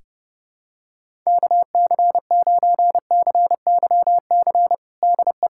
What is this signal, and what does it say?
Morse code, heard as a keyed tone, about 32 wpm, 700 Hz. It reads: KC9CYC DE
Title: USA Callsign DE Callsign 30wpm